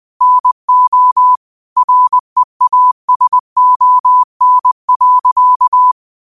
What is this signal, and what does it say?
Morse code, heard as a keyed tone, about 20 wpm, 1000 Hz. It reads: NO REASON.
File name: qotdfiles / 20wpm